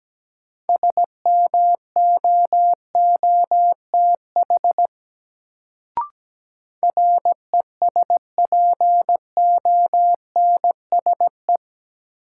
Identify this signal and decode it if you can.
Morse code, heard as a keyed tone, about 17 wpm, 700 Hz. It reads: SMOOTH RESPONSE
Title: Sentences from Top 1400 Words - New Word Drill Alt - 17wpm